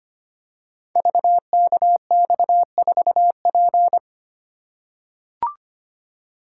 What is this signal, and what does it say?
Morse code, heard as a keyed tone, about 25 wpm, 700 Hz. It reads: VKX4P
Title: Code Groups - 5 Letter:Numbers - 25wpm